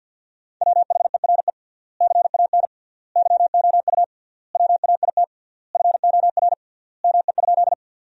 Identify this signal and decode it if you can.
Morse code, heard as a keyed tone, about 50 wpm, 700 Hz. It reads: WHERE CAN YOU WAIT FOR ME?
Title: 3x Word-Spacing - Sentences from Top 500 Words - 50wpm